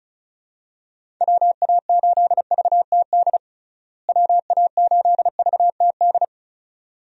Morse code, heard as a keyed tone, about 34 wpm, 700 Hz. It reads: WA8VTD WA8VTD
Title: USA Call Signs - 6 Characters - Repeated 2x - 35wpm